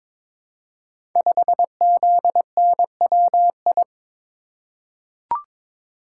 Morse code, heard as a keyed tone, about 22 wpm, 700 Hz. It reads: 5ZNWI
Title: Code Groups - 5 Letter:Numbers - 22wpm